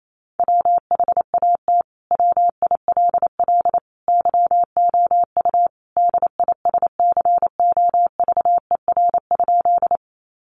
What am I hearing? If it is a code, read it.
Morse code, heard as a keyed tone, about 28 wpm, 700 Hz. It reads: WHAT WILL YOU DISCOVER?
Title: Sentences from Top 1100 Words 28wpm